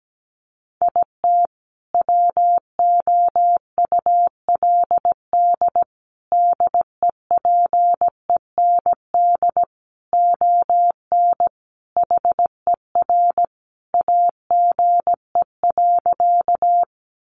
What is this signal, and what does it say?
Morse code, heard as a keyed tone, about 17 wpm, 700 Hz. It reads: IT WOULD DEPEND ON HER AGE.
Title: Sentences from Top 1300 Words 17wpm